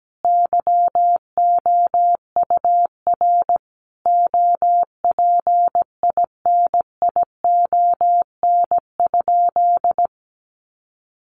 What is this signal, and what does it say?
Morse code, heard as a keyed tone, about 17 wpm, 700 Hz. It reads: YOUR OPINION?
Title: Sentences from Top 1200 Words 17wpm